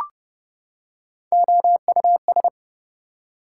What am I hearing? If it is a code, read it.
Morse code, heard as a keyed tone, about 30 wpm, 700 Hz. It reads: OUS
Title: Prefixes and Suffixes 30wpm